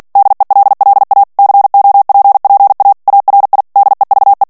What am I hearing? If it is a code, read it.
Morse code, heard as a keyed tone, about 48 wpm, 800 Hz. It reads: DELLA COPPA ARI DEVE
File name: ITALIEN texte clair 2 vitesse 45 mots minute